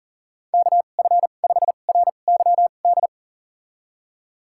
Morse code, heard as a keyed tone, about 40 wpm, spaced as for 29 wpm, 700 Hz. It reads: KF5RYD
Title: USA Call Signs - 6 Characters - 2x Character Spacing - 40wpm